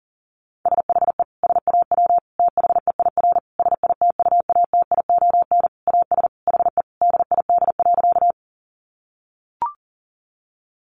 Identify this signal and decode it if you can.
Morse code, heard as a keyed tone, about 40 wpm, 700 Hz. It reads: SHE SAW THEIR SITUATION AS HE DID.
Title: Sentences from Top 1200 Words 40wpm